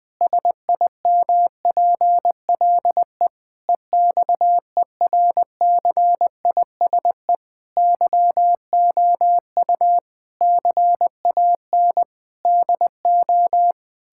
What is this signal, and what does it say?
Morse code, heard as a keyed tone, about 20 wpm, 700 Hz. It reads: SIMPLE EXERCISE YOU CAN DO
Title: Difficult Sentences with Hints - 20wpm